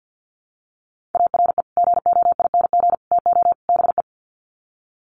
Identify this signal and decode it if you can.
Morse code, heard as a keyed tone, about 50 wpm, 700 Hz. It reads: ARE GOING TO BE
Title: Sets of 4 Words 50wpm